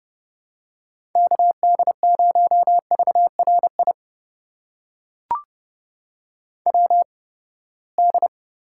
Morse code, heard as a keyed tone, about 30 wpm, spaced as for 21 wpm, 700 Hz. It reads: KD0VRI WD
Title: USA Call Signs - 6 Characters - 8x Character Spacing - 30wpm